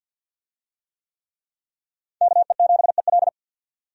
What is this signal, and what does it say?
Morse code, heard as a keyed tone, about 52 wpm, 700 Hz. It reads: KE7EL
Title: USA Call Signs - 5 Characters - 50wpm